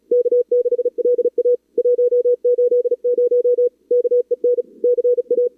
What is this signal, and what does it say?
Morse code, heard as a keyed tone, about 40 wpm, 440 Hz. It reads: K6LA 180 KEN CA
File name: CWSprint / Feb2022 / k5tr-feb-na-sprint-cw-k5ot -0100